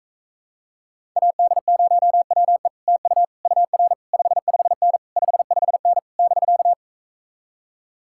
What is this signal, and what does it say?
Morse code, heard as a keyed tone, about 46 wpm, 700 Hz. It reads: AD0WE TU UR 55N 55N <BK>